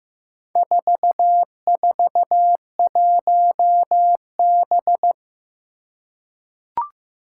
Morse code, heard as a keyed tone, about 15 wpm, 700 Hz. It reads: 441B
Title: Four Challenging Letter-Number Combinations 15wpm